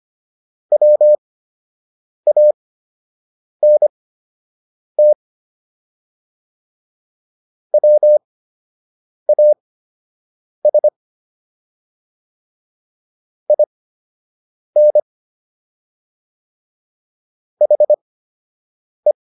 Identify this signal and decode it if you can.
Morse code, heard as a keyed tone, about 25 wpm, spaced as for 7 wpm, 600 Hz. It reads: WANT WAS IN HE